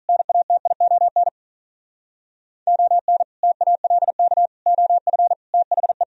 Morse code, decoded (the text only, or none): NATION ON TALK OF THE